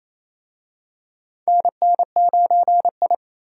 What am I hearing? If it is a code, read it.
Morse code, heard as a keyed tone, about 28 wpm, 700 Hz. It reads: NN9I
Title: USA Call Signs - 4 Characters - Repeated 2x - 28wpm